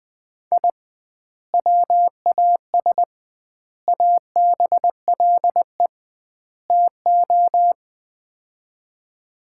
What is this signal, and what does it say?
Morse code, heard as a keyed tone, about 20 wpm, 700 Hz. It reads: I WAS ABLE TO
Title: 2x Word-Spacing - Sets of 4 Words - 20wpm